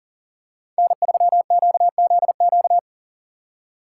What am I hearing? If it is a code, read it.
Morse code, heard as a keyed tone, about 40 wpm, 700 Hz. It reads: N3QZQ
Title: USA Call Signs - 5 Characters - 8x Character Spacing - 40wpm